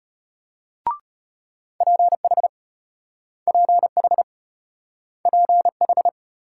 Morse code, heard as a keyed tone, about 33 wpm, 700 Hz. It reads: PH PH PH